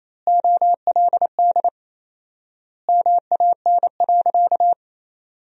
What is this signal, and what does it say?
Morse code, heard as a keyed tone, about 28 wpm, 700 Hz. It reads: OLD MAN.
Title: 4x Word-Spacing - Sentences from Top 500 Words - 28wpm